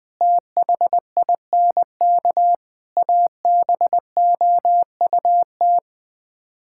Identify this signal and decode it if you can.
Morse code, heard as a keyed tone, about 20 wpm, 700 Hz. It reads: THINK ABOUT
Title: Sets of 8 Words - 20wpm